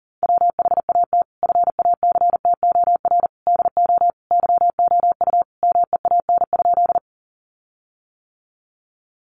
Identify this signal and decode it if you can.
Morse code, heard as a keyed tone, about 37 wpm, 700 Hz. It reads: WHAT FACTOR DO YOU MEAN?
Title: Sentences from Top 1300 Words 40wpm